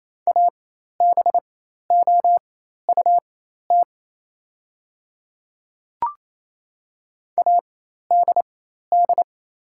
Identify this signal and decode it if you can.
Morse code, heard as a keyed tone, about 27 wpm, spaced as for 13 wpm, 690 Hz. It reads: ABOUT ADD
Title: 4x Character Spacing - Top 200 Words - 28wpm